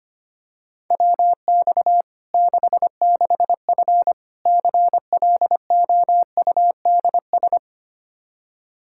Morse code, heard as a keyed tone, about 25 wpm, 700 Hz. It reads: WX 66F CLOUDS
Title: Practice QSO - Exchange 2 - Info - 25wpm